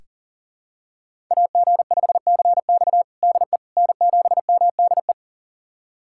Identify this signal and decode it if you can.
Morse code, heard as a keyed tone, about 40 wpm, 700 Hz. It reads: AG5CX DE N7MDE
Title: USA Callsign DE Callsign - 5 Characters - 40wpm